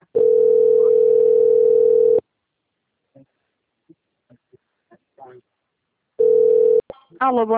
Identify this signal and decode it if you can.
Ringback tone (North America), 440 + 480 Hz, with speech.
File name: out- - - - - .40063